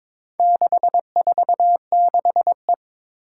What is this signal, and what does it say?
Morse code, heard as a keyed tone, about 22 wpm, 700 Hz. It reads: 646E
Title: Four Challenging Letter-Number Combinations 22wpm